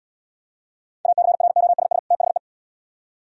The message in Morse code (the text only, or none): ACROSS THE